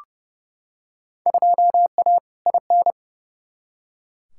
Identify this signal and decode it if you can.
Morse code, heard as a keyed tone, about 31 wpm, 700 Hz. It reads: 2A IN
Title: Field Day Exchanges - 30wpm